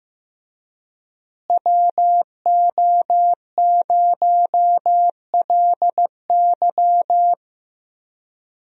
Morse code, heard as a keyed tone, about 15 wpm, 700 Hz. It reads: WO0LY
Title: USA Call Signs - 5 Characters - Repeated 3x - 15wpm